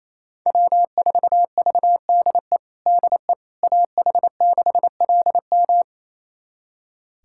Morse code, heard as a keyed tone, about 28 wpm, 700 Hz. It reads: W4VDE DE AH6LM
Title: USA Callsign DE Callsign - 5 Characters - 28wpm